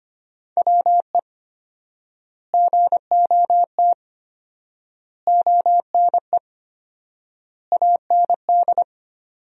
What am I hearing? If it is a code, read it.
Morse code, heard as a keyed tone, about 25 wpm, 700 Hz. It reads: WE GOT ONE AND
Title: 4x Word-Spacing - Sentences from Top 300 Words - 25wpm